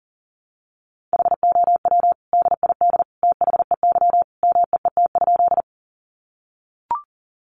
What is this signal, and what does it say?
Morse code, heard as a keyed tone, about 40 wpm, 700 Hz. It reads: HOW DID THEY MEET?
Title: Sentences from Top 1100 Words 40wpm